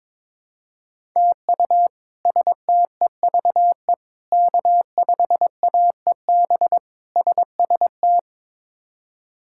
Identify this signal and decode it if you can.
Morse code, heard as a keyed tone, about 22 wpm, 700 Hz. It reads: TU STEVE K5AEB SST
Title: K1USN SST QSO Practice - 22wpm